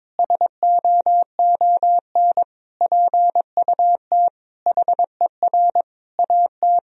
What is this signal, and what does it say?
Morse code, heard as a keyed tone, about 22 wpm, 700 Hz. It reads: SOON PUT HER AT
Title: Sentences from Top 1000 Words - Review Part 1 - 22wpm